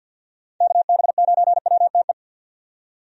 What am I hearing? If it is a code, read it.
Morse code, heard as a keyed tone, about 53 wpm, 700 Hz. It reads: KB9WTE
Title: USA Call Signs - 6 Characters - 50wpm